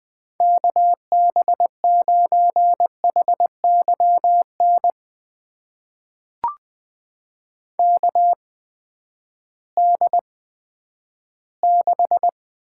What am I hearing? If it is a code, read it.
Morse code, heard as a keyed tone, about 20 wpm, spaced as for 11 wpm, 700 Hz. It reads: KB9HYN KD6
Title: USA Call Signs - 6 Characters - 8x Character Spacing - 20wpm